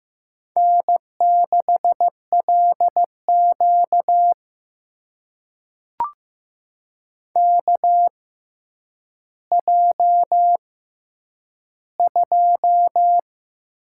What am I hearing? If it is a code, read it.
Morse code, heard as a keyed tone, about 15 wpm, spaced as for 9 wpm, 700 Hz. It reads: N6LQ KJ2
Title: USA Call Signs - 4 Characters - 6x Character Spacing - 15wpm